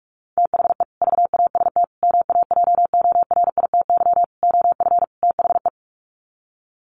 Morse code, heard as a keyed tone, about 44 wpm, 700 Hz. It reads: THE VAST MAJORITY OF THE
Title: Sets of 5 Words Encore 45wpm